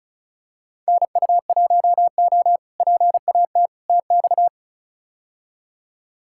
Morse code, heard as a keyed tone, about 36 wpm, 700 Hz. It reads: NU1O PAT TX
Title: K1USN SST QSO Practice - 35wpm